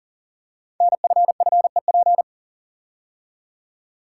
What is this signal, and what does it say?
Morse code, heard as a keyed tone, about 39 wpm, 690 Hz. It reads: NFFEP
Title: Code Groups - 5 Letters - 40wpm